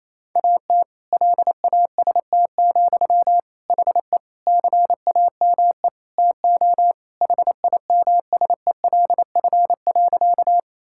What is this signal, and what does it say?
Morse code, heard as a keyed tone, about 28 wpm, 700 Hz. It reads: AT LAST, HE CAME TO HIMSELF.
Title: Sentences from Top 1600 Words 28wpm